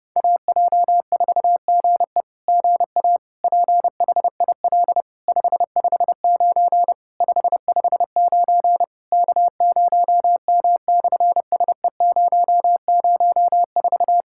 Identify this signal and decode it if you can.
Morse code, heard as a keyed tone, about 30 wpm, 700 Hz. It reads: AJ4GE GA PHIL 559 559 K0M/SE004
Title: SOTA QSO Practice 30wpm